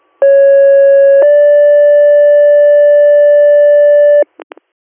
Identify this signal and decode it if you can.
Two-tone page, heard as a one-second tone, then a three-second tone, about 550 then 580 Hz. It is heard from 0.2 to 4.2 s.